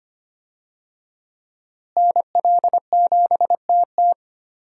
Morse code, heard as a keyed tone, about 25 wpm, 700 Hz. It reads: NL7TT
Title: USA Call Signs - 5 Characters - Repeated 2x - 25wpm